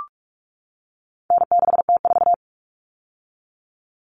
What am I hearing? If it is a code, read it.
Morse code, heard as a keyed tone, about 44 wpm, 700 Hz. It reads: N6T4
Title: All Cut Numbers - 4-Digit 45wpm